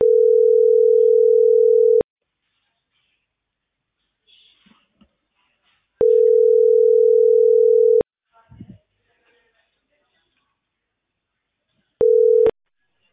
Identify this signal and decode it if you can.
Ringback tone (North America), 440 + 480 Hz.
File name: outgoing recordings